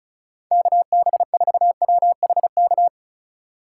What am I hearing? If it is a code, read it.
Morse code, heard as a keyed tone, about 34 wpm, 700 Hz. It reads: KB4WHK
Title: USA Call Signs - 6 Characters - 4x Character Spacing - 35wpm